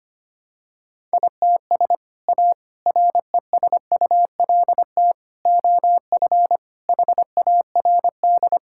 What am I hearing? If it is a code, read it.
Morse code, heard as a keyed tone, about 25 wpm, 700 Hz. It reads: ITS A RESULT OF HARD
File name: practice / 25wpm / Sentences with 3-Word Phrases - 25wpm